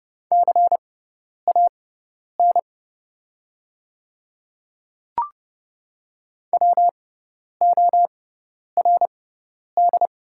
Morse code, heard as a keyed tone, about 30 wpm, spaced as for 10 wpm, 690 Hz. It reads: CAN WORD
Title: 6x Character Spacing - Top 100 Words - 30wpm